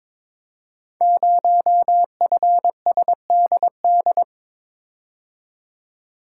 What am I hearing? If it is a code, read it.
Morse code, heard as a keyed tone, about 22 wpm, 700 Hz. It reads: 0FSDD
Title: Code Groups - 5 Letter:Numbers - 22wpm